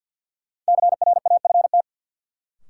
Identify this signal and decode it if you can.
Morse code, heard as a keyed tone, about 49 wpm, 700 Hz. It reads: CRAFT
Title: Sentences from Top 1000 Words - Review Part 6 - New Word Drill - 50wpm